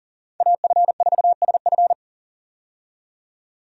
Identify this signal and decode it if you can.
Morse code, heard as a keyed tone, about 40 wpm, 700 Hz. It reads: AF4SF
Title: USA Call Signs - 5 Characters - 40wpm